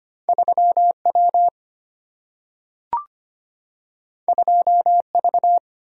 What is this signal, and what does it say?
Morse code, heard as a keyed tone, about 25 wpm, 700 Hz. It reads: 3W 2V
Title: Single-Letter Single-Number 25wpm